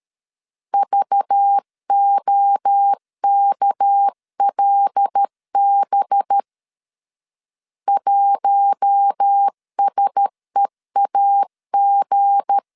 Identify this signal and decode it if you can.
Morse code, heard as a keyed tone, about 13 wpm, 800 Hz. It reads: VOKLB 1SEAG
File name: LEZIONI / 7 GF9Z / 7 G 12wpm 2055